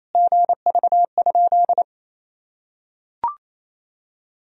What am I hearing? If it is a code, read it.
Morse code, heard as a keyed tone, about 28 wpm, 700 Hz. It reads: GV?
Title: Sentences from Top 200 Words and Abbreviations 28wpm